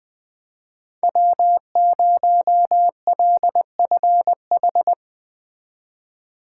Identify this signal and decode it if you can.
Morse code, heard as a keyed tone, about 20 wpm, 700 Hz. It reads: W0LFH
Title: Speed Racing - USA Call Signs 17wpm